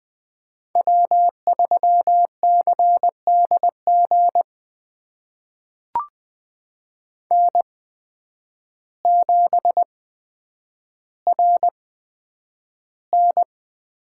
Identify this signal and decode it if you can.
Morse code, heard as a keyed tone, about 20 wpm, spaced as for 9 wpm, 700 Hz. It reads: W3CDG N7RN
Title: USA Call Signs - 8x Character Spacing - 20wpm